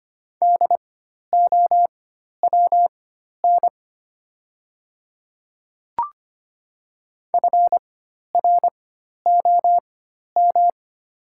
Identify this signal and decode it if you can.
Morse code, heard as a keyed tone, about 25 wpm, spaced as for 12 wpm, 700 Hz. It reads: DOWN FROM
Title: 4x Character Spacing - Top 100 Words - 25wpm